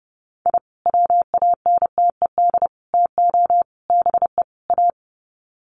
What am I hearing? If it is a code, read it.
Morse code, heard as a keyed tone, about 30 wpm, 700 Hz. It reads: I WANTED TO BE A
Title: Sets of 5 Words Encore2 30wpm